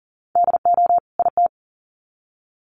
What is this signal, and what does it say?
Morse code, heard as a keyed tone, about 40 wpm, 700 Hz. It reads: DO IT
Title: Sets of 2 Words 40wpm